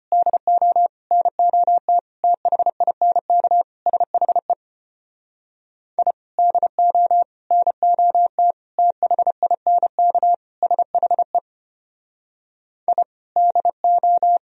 Morse code, heard as a keyed tone, about 32 wpm, 700 Hz. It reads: DO NOT THINK SHE I DO NOT THINK SHE I DO